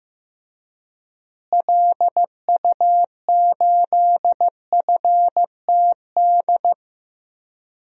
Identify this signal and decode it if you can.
Morse code, heard as a keyed tone, about 15 wpm, 700 Hz. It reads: LU8FTD
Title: DXCC-100 Call Signs - 6 Characters - Repeated 2x - 15wpm